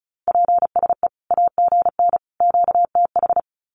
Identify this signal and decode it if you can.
Morse code, heard as a keyed tone, about 34 wpm, 700 Hz. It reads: PSE AGN QTH
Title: Common QSO Phrases 35wpm